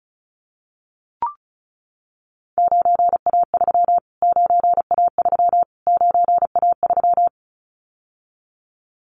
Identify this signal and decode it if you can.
Morse code, heard as a keyed tone, about 34 wpm, 700 Hz. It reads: 9A3 9A3 9A3
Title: Callsign Prefixes for Top 25 DXCC - Repeated 3x - 35wpm